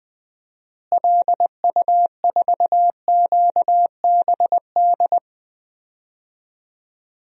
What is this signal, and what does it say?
Morse code, heard as a keyed tone, about 20 wpm, 700 Hz. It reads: LU4QBD
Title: DXCC-100 Call Signs - 6 Characters - Repeated 3x - 20wpm